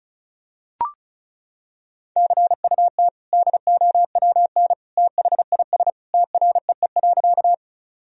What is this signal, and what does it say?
Morse code, heard as a keyed tone, about 34 wpm, 690 Hz. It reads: CUT DOWN THIS TREE.
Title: Sentences from Top 1000 Words - Review Part 1 - 35wpm